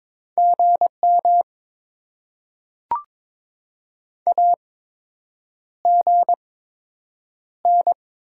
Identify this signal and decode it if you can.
Morse code, heard as a keyed tone, about 22 wpm, spaced as for 8 wpm, 700 Hz. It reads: GM AGN
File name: Top 20 QSO Elements - 8x Character Spacing - 22wpm